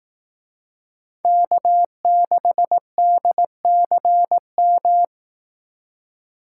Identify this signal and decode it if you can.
Morse code, heard as a keyed tone, about 18 wpm, 700 Hz. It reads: K6DCM
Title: Speed Racing - USA 1x3 Call Signs 15wpm